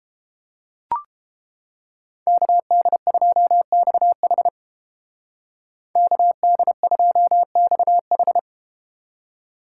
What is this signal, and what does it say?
Morse code, heard as a keyed tone, about 33 wpm, 700 Hz. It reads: KD2XH KD2XH